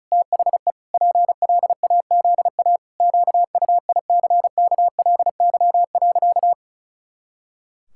Morse code, heard as a keyed tone, about 36 wpm, 700 Hz. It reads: THE PLAZA QUICKLY.
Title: Pangram Challenge v2 - 35wpm